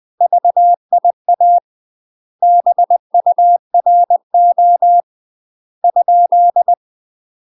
Morse code, heard as a keyed tone, about 20 wpm, 700 Hz. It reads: VIA BURO ?
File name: UK MASTER QSOs TRAINING / 20WPM / UK MASTER QSOs 0065